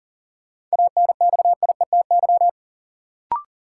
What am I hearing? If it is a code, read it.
Morse code, heard as a keyed tone, about 38 wpm, 700 Hz. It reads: ANXIETY
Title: Sentences from Top 1600 Words - New Word Drill Alt - 40wpm